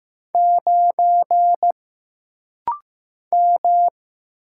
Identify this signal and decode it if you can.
Morse code, heard as a keyed tone, about 15 wpm, 700 Hz. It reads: 9 M
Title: Single Letter-Number Alt - Rapid-Fire 15wpm